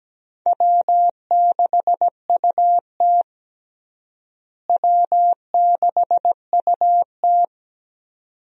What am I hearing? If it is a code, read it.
Morse code, heard as a keyed tone, about 17 wpm, 700 Hz. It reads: W6UT W6UT
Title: USA Call Signs - 4 Characters - Repeated 2x - 17wpm